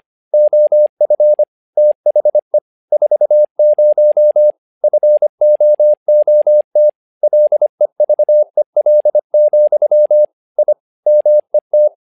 Morse code, heard as a keyed tone, about 25 wpm, 600 Hz. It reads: OF THE 40 FOOT LEVEL, I MET
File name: learning center / tower trouble / tower trouble 25wpm